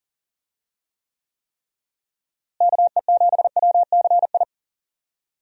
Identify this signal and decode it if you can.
Morse code, heard as a keyed tone, about 40 wpm, 700 Hz. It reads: KE7WCI
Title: USA Call Signs - 6 Characters - Repeated 3x - 40wpm